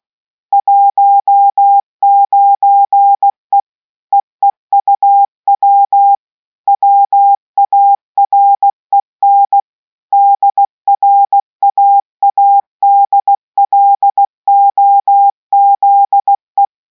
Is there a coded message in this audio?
Yes: Morse code from 0.5 to 16.6 s, heard as a keyed tone, about 16 wpm, 800 Hz. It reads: 19E EEUW WAREN DRAADLOZE